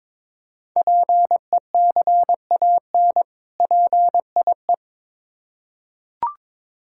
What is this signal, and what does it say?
Morse code, heard as a keyed tone, about 22 wpm, 700 Hz. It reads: PECAN PIE
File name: Difficult Words with Hints - 22wpm